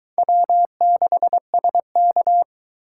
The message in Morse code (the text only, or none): W6SK